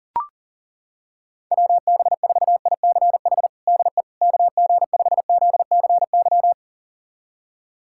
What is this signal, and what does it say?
Morse code, heard as a keyed tone, about 40 wpm, 700 Hz. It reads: WB4ICH DE KG5ZCY
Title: USA Callsign DE Callsign - 6 Characters - 40wpm